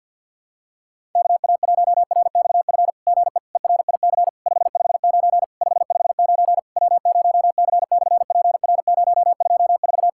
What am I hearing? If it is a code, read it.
Morse code, heard as a keyed tone, about 50 wpm, 700 Hz. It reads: KA1RXF GE ERIC 559 559 W0C/PR024
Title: SOTA QSO Practice 50wpm